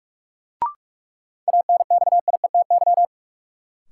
Morse code, heard as a keyed tone, about 46 wpm, 700 Hz. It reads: ANXIETY
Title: Sentences from Top 1600 Words - New Word Drill Alt - 45wpm